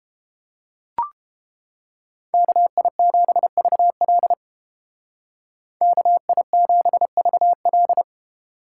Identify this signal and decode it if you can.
Morse code, heard as a keyed tone, about 30 wpm, 700 Hz. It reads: KI7VL KI7VL